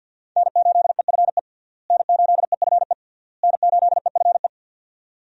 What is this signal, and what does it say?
Morse code, heard as a keyed tone, about 52 wpm, 700 Hz. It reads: N8EFE N8EFE N8EFE
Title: USA Call Signs - 5 Characters - Repeated 3x - 50wpm